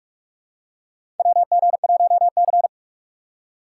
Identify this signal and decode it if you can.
Morse code, heard as a keyed tone, about 42 wpm, 700 Hz. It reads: WG1C
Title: USA Call Signs - 4 Characters - 4x Character Spacing - 45wpm